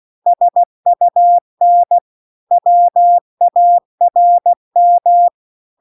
Morse code, heard as a keyed tone, about 16 wpm, 700 Hz. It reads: SUN WARM